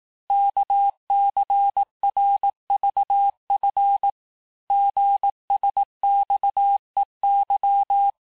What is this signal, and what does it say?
Morse code, heard as a keyed tone, about 18 wpm, 800 Hz. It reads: KCRVF GSXEY
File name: cours-du-lundi files / cours-1852-texte-chiffre-vitesse-17MPM